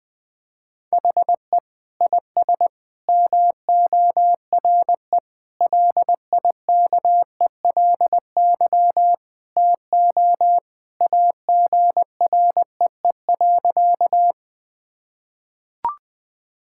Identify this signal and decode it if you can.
Morse code, heard as a keyed tone, about 20 wpm, 700 Hz. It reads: HE IS MORE LIKELY TO AGREE.
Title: Sentences with 3-Word Phrases - Spoken Beginning - 20wpm